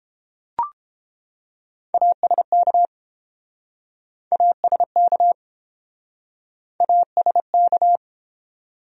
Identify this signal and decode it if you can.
Morse code, heard as a keyed tone, about 31 wpm, 700 Hz. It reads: ASK ASK ASK